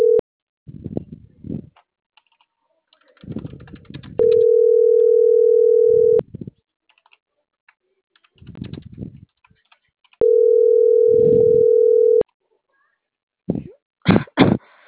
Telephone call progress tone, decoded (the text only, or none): ringback tone